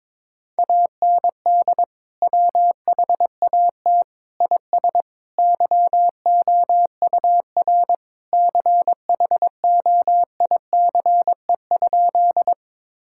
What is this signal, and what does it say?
Morse code, heard as a keyed tone, about 22 wpm, 700 Hz. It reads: AND WHAT IS YOUR CHOICE?
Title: Sentences from Top 1000 Words - Review Part 6 - 22wpm